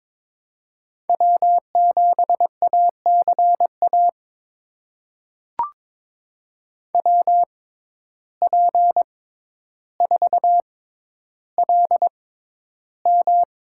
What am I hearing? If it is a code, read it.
Morse code, heard as a keyed tone, about 22 wpm, spaced as for 11 wpm, 700 Hz. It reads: W7ACA WP4LM
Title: USA Call Signs - 5 Characters - 6x Character Spacing - 22wpm